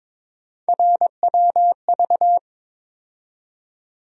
Morse code, heard as a keyed tone, about 22 wpm, 700 Hz. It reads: RWV
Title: USA Call Sign Postfixes - 3 Letters 22wpm